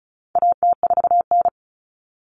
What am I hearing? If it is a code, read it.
Morse code, heard as a keyed tone, about 34 wpm, 700 Hz. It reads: AT4N
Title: Common Cut Numbers - 4-Digit 35wpm